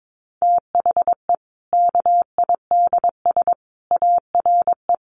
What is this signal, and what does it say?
Morse code, heard as a keyed tone, about 22 wpm, 700 Hz. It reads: THE KIDS ARE
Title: Easy Phrases 22wpm